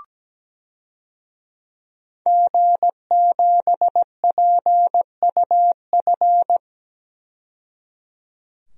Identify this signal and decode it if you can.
Morse code, heard as a keyed tone, about 17 wpm, 700 Hz. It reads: G7PUF
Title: DXCC-100 Call Signs - 5 Characters - 17wpm